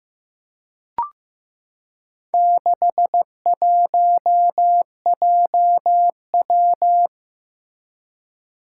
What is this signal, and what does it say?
Morse code, heard as a keyed tone, about 15 wpm, 700 Hz. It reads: 61JW
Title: Four Challenging Letter-Number Combinations 15wpm